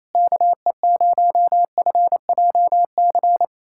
Morse code, heard as a keyed tone, about 28 wpm, 700 Hz. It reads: KE0FJC
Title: USA Callsign DE Callsign - 6 Characters - 28wpm